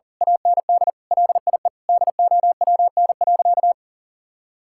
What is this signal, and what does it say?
Morse code, heard as a keyed tone, about 40 wpm, 700 Hz. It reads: AND LIE DOWN.
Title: Sentences from Top 1600 Words 40wpm